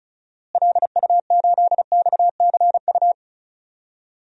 Morse code, heard as a keyed tone, about 36 wpm, 700 Hz. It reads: LU8XCU
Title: DXCC-100 Call Signs - 6 Characters - Repeated 2x - 35wpm